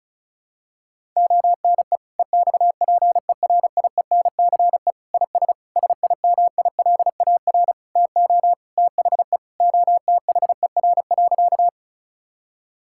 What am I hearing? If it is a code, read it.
Morse code, heard as a keyed tone, about 36 wpm, 700 Hz. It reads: ONE EXPERIENCE IS SIMILAR TO THE OTHER.
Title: Sentences from Top 1000 Words - Review Part 5 - 35wpm